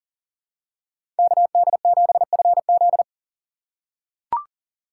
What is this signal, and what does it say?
Morse code, heard as a keyed tone, about 40 wpm, 700 Hz. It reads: KD7FZ
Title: USA Call Signs - 5 Characters - 6x Character Spacing - 40wpm